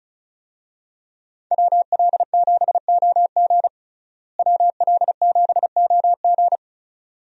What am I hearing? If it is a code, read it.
Morse code, heard as a keyed tone, about 34 wpm, 700 Hz. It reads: WL7OG WL7OG
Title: USA Call Signs - 5 Characters - Repeated 2x - 35wpm